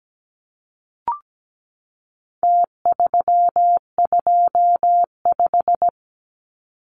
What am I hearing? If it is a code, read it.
Morse code, heard as a keyed tone, about 17 wpm, 700 Hz. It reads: T325
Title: Numbers - Contest Style - 4-Digit 17wpm